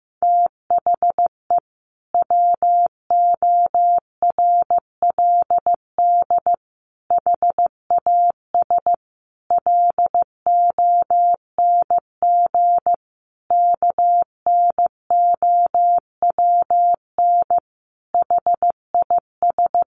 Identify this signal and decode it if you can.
Morse code, heard as a keyed tone, about 15 wpm, 700 Hz. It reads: THE WORLD HAS LONG KNOWN HIS
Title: Sentences from Top 1200 Words 15wpm